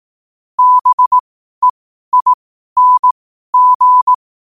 Morse code, heard as a keyed tone, about 18 wpm, spaced as for 12 wpm, 1000 Hz. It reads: BEING
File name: qotdfiles / 12wpm